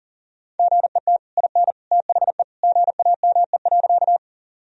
Morse code, heard as a keyed tone, about 37 wpm, 700 Hz. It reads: GET IN THE GAME.
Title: Sentences from Top 1000 Words - Review Part 3 - 40wpm